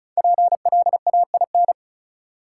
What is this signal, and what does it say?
Morse code, heard as a keyed tone, about 34 wpm, 700 Hz. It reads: PLAIN
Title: Sentences from Top 400 Words - New Word Drill - 35wpm